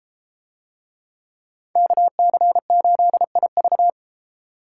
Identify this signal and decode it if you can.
Morse code, heard as a keyed tone, about 32 wpm, 700 Hz. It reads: KC8IV